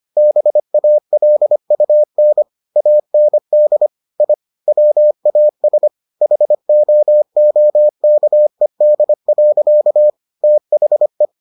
Morse code, heard as a keyed tone, about 25 wpm, 600 Hz. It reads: BALUN AND I WAS HOOKED. THE